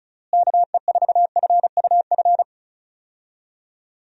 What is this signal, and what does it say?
Morse code, heard as a keyed tone, about 36 wpm, 700 Hz. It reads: KE4FUF
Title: USA Call Signs - 6 Characters - Repeated 2x - 35wpm